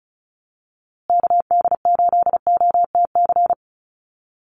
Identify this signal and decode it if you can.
Morse code, heard as a keyed tone, about 34 wpm, 700 Hz. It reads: KD8OTC
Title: USA Call Signs Repeated 3x 35wpm